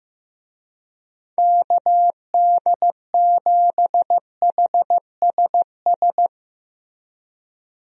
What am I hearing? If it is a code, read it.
Morse code, heard as a keyed tone, about 15 wpm, 700 Hz. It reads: KD7HSS
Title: USA Call Signs - 6 Characters - Repeated 2x - 15wpm